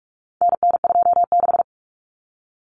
Morse code, heard as a keyed tone, about 44 wpm, 700 Hz. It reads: NN26
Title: Common Cut Numbers - 4-Digit 45wpm